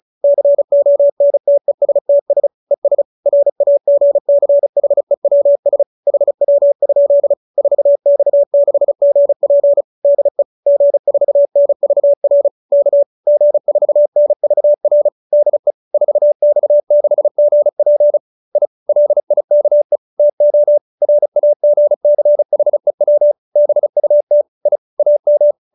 Morse code, heard as a keyed tone, about 34 wpm, 570 Hz. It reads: CONTESTS ES RAGCHEWS HW? 4X6GP DE G4NVR K G4NVR DE 4X6GP I LIKE TO RAGCHEW BUT I AM